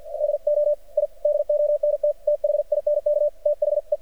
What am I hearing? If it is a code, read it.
Morse code, heard as a keyed tone, about 48 wpm, 590 Hz. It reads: DO I DONT THINK THE
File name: archived / CRSnet